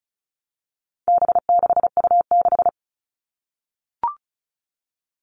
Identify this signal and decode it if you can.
Morse code, heard as a keyed tone, about 34 wpm, 700 Hz. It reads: B6U6
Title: All Cut Numbers - 4-Digit 35wpm